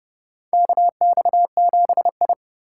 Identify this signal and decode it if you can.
Morse code, heard as a keyed tone, about 30 wpm, 700 Hz. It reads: KX7I
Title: USA Call Signs - 4 Characters - 4x Character Spacing - 30wpm